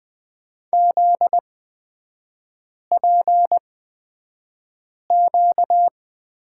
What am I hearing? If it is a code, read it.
Morse code, heard as a keyed tone, about 20 wpm, spaced as for 5 wpm, 700 Hz. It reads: ZPQ
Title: 243 Lesson 39e - Letter z - Three Characters 20wpm